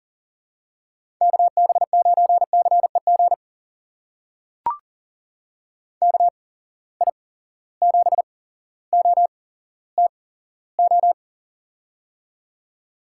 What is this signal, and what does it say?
Morse code, heard as a keyed tone, about 40 wpm, spaced as for 17 wpm, 700 Hz. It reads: KB9CEG KI7OTO
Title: USA Call Signs - 6 Characters - 8x Character Spacing - 40wpm